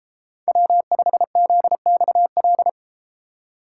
Morse code, heard as a keyed tone, about 32 wpm, 700 Hz. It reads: W5ZXL